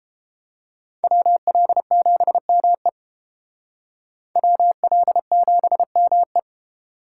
Morse code, heard as a keyed tone, about 29 wpm, 700 Hz. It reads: WL7ME WL7ME